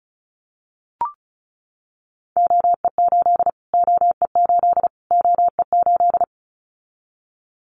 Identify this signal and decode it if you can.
Morse code, heard as a keyed tone, about 34 wpm, 700 Hz. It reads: OE8 OE8 OE8
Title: Callsign Prefixes for Top 25 DXCC - Repeated 3x - 35wpm